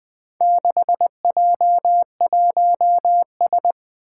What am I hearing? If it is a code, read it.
Morse code, heard as a keyed tone, about 20 wpm, 700 Hz. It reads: 6J1S
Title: Four Challenging Letter-Number Combinations 20wpm